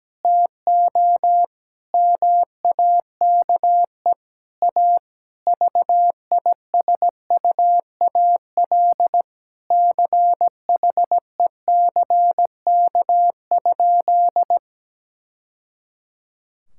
Morse code, heard as a keyed tone, about 17 wpm, 700 Hz. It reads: TO MAKE A VISUAL CHECK?
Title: Sentences from Top 1000 Words - Review Part 6 - 17wpm